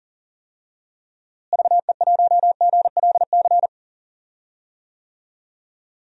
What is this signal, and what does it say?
Morse code, heard as a keyed tone, about 40 wpm, 700 Hz. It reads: VE1GLC
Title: DXCC-100 Call Signs - 6 Characters - Repeated 2x - 40wpm